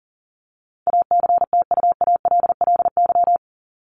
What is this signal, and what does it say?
Morse code, heard as a keyed tone, about 40 wpm, 700 Hz. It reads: ACTUALLY
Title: LY Suffix - Top 100 Words with Base Words 40wpm